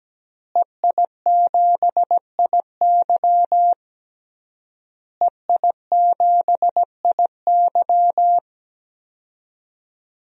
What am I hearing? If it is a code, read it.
Morse code, heard as a keyed tone, about 17 wpm, 700 Hz. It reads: EI7IY EI7IY
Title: DXCC-100 Call Signs - 5 Characters - Repeated 3x - 17wpm